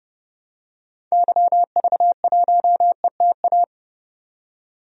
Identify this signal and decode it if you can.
Morse code, heard as a keyed tone, about 30 wpm, 700 Hz. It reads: YV1ETA
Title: DXCC-100 Call Signs - 6 Characters - 30wpm